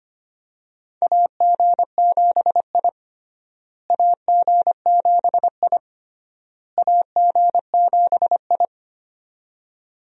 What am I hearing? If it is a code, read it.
Morse code, heard as a keyed tone, about 25 wpm, 700 Hz. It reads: AG7I AG7I AG7I
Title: USA Call Signs - 4 Characters - Repeated 3x - 25wpm